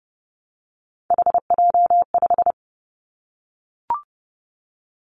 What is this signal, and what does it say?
Morse code, heard as a keyed tone, about 30 wpm, 700 Hz. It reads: HJ5
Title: Callsign Prefixes for Top 100 DXCC - 30wpm